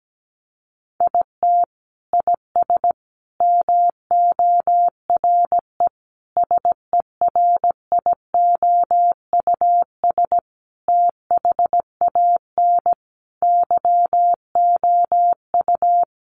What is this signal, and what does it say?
Morse code, heard as a keyed tone, about 17 wpm, 700 Hz. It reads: IT IS MORE SERIOUS THAN YOU
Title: Sentences from Top 1100 Words 17wpm